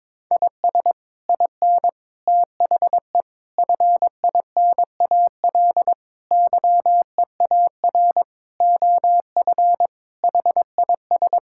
Morse code, heard as a keyed tone, about 21 wpm, 700 Hz. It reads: IS IN THE FINAL YEAR OF HIS
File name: practice / 22wpm / Sets of 8 Words - 22wpm